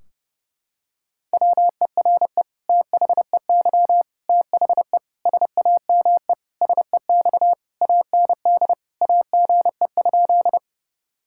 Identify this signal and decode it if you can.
Morse code, heard as a keyed tone, about 30 wpm, 700 Hz. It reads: WERE THEY THE SAME SEX AND AGE?
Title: Sentences from Top 1600 Words 30wpm